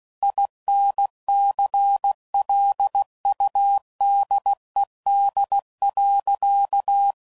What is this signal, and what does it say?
Morse code, heard as a keyed tone, about 16 wpm, 800 Hz. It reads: INCLUDED.
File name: Sayings / Quotes / Airtight